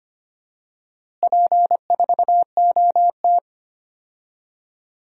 Morse code, heard as a keyed tone, about 25 wpm, 700 Hz. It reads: P4OT
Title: DXCC-100 Call Signs - 4 Characters - Repeated 2x - 25wpm